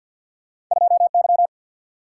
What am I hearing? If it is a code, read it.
Morse code, heard as a keyed tone, about 51 wpm, 700 Hz. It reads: 2Y
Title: Single-Letter Single-Number 50wpm